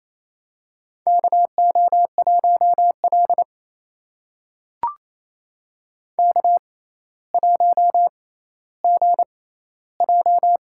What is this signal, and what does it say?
Morse code, heard as a keyed tone, about 28 wpm, spaced as for 14 wpm, 700 Hz. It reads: KO1L K1GJ
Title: USA Call Signs - 6x Character Spacing - 28wpm